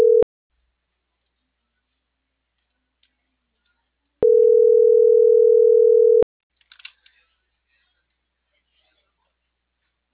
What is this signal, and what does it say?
Ringback tone (North America), 440 + 480 Hz.